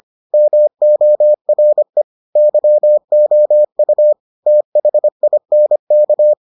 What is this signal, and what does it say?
Morse code, heard as a keyed tone, about 25 wpm, 600 Hz. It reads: MORE YOU THINK